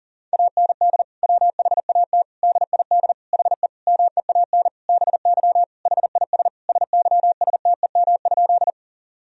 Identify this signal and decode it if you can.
Morse code, heard as a keyed tone, about 40 wpm, 700 Hz. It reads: AND WHAT DID HE MEAN BY HIS SYSTEM?
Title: Sentences from Top 1000 Words - Review Part 3 - 40wpm